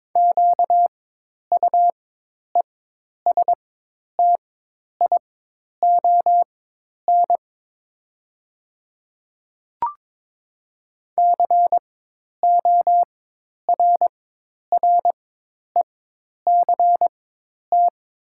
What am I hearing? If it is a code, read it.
Morse code, heard as a keyed tone, about 21 wpm, spaced as for 10 wpm, 700 Hz. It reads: QUESTION CORRECT
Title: 4x Character Spacing - Top 500 Words - 22wpm